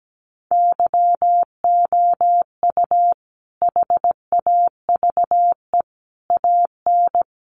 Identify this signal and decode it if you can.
Morse code, heard as a keyed tone, about 17 wpm, 700 Hz. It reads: YOU HAVE AN
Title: Sentences from Top 1300 Words 17wpm